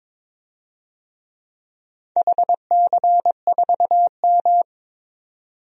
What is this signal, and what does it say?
Morse code, heard as a keyed tone, about 22 wpm, 700 Hz. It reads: HC4M
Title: DXCC-100 Call Signs - 4 Characters - 22wpm